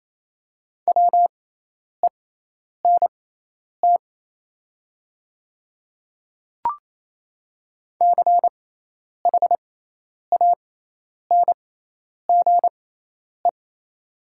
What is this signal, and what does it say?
Morse code, heard as a keyed tone, about 28 wpm, spaced as for 10 wpm, 700 Hz. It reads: WENT CHANGE
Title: 6x Character Spacing - Top 300 Words - 28wpm